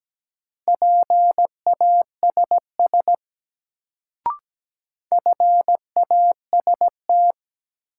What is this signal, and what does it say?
Morse code, heard as a keyed tone, about 17 wpm, 700 Hz. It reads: PASS FAST
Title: Sentences from Top 400 Words - New Word Drill - 17wpm